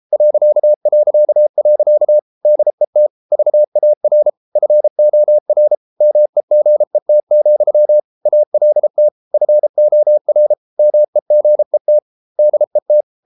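Morse code, heard as a keyed tone, about 33 wpm, 600 Hz. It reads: ... DET VAR FOR MEGET, ALT FOR MEGET DET
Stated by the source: Morse code at 33 wpm, QRQ.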